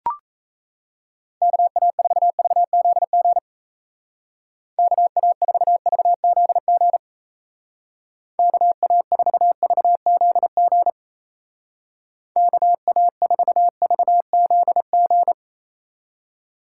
Morse code, heard as a keyed tone, about 35 wpm, 700 Hz. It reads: KA4VZG KA4VZG KA4VZG KA4VZG
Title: Speed Racing - USA Call Signs 28wpm